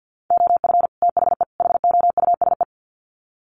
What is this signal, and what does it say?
Morse code, heard as a keyed tone, about 50 wpm, 700 Hz. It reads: OF THE HOUSE
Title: Sets of 3 Words 50wpm